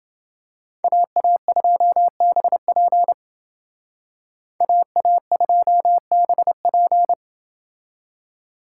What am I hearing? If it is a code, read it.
Morse code, heard as a keyed tone, about 28 wpm, 700 Hz. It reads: AA2BP AA2BP